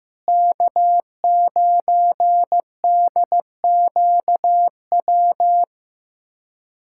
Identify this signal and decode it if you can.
Morse code, heard as a keyed tone, about 15 wpm, 700 Hz. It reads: K9DQW
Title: USA Callsign DE Callsign 15wpm